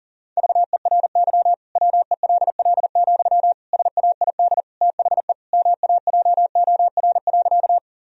Morse code, heard as a keyed tone, about 40 wpm, 700 Hz. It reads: VERY WELL, SAID THE MAJOR.
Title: Sentences from Top 1000 Words - Review Part 4 - 40wpm